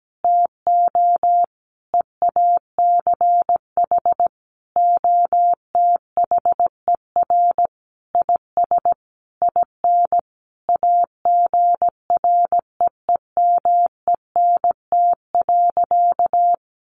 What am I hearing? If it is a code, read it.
Morse code, heard as a keyed tone, about 17 wpm, 700 Hz. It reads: TO EACH OTHER IS IN AGREEMENT.
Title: Sentences from Top 1500 Words 17wpm